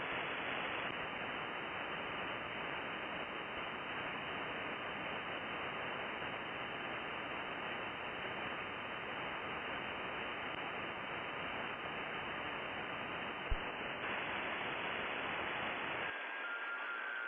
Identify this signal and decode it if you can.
FT8.
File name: contests / sept-vhf-k5tr- -2200